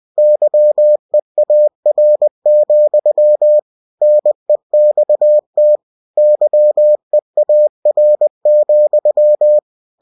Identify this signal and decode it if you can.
Morse code, heard as a keyed tone, about 20 wpm, 600 Hz. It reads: YEAR, NEXT YEAR,
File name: cw-books / winnie-the-pooh / winnie-20wpm-page-0078